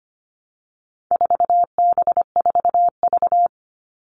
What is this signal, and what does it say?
Morse code, heard as a keyed tone, about 25 wpm, 700 Hz. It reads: 4B4V
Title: All Cut Numbers - 4-Digit 25wpm